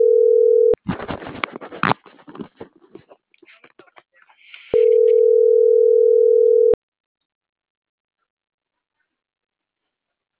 Ringback tone (North America), 440 + 480 Hz.